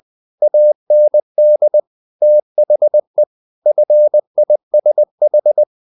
Morse code, heard as a keyed tone, about 20 wpm, 600 Hz. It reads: AND THE FISH
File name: the old man and the sea part 2 20wpm